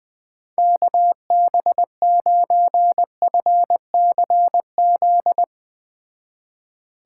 Morse code, heard as a keyed tone, about 20 wpm, 700 Hz. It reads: KB9FCZ
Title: Speed Racing - USA Call Signs 17wpm